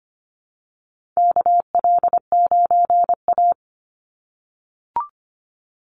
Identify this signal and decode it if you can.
Morse code, heard as a keyed tone, about 25 wpm, 700 Hz. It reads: KL9A
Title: Call Signs - 25wpm